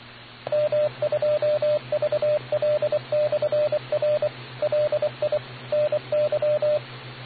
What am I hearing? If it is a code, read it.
Morse code, heard as a keyed tone, about 24 wpm, 600 Hz, with radio noise. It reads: M2VL/R LI NY